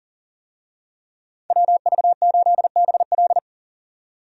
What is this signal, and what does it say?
Morse code, heard as a keyed tone, about 37 wpm, 700 Hz. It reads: WV8BL
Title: USA Call Signs - 5 Characters - 40wpm